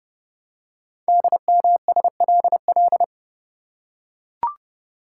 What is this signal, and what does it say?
Morse code, heard as a keyed tone, about 30 wpm, 700 Hz. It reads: DMSLL
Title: Code Groups - 5 Letter:Numbers - 30wpm